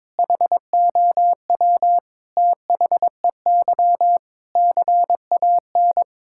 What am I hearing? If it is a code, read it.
Morse code, heard as a keyed tone, about 21 wpm, 700 Hz. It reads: HOW THEY CAN
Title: Sentences from Top 1000 Words - Review Part 5 - 22wpm